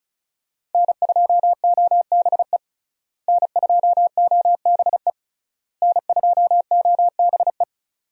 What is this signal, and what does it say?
Morse code, heard as a keyed tone, about 34 wpm, 700 Hz. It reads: N2OBE N2OBE N2OBE
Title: USA Call Signs - 5 Characters - Repeated 3x - 35wpm